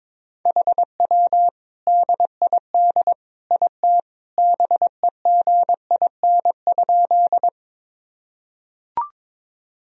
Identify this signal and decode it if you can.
Morse code, heard as a keyed tone, about 22 wpm, 700 Hz. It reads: HW DID IT BEGIN?
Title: Sentences from Top 300 Words and Abbreviations 22wpm